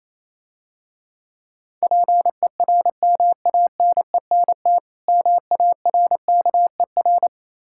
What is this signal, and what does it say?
Morse code, heard as a keyed tone, about 28 wpm, 690 Hz. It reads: PERMANENT MARKER